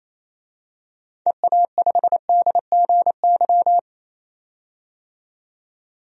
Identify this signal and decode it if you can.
Morse code, heard as a keyed tone, about 28 wpm, 700 Hz. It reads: EA5DGY
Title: DXCC-100 Call Signs - 6 Characters - 28wpm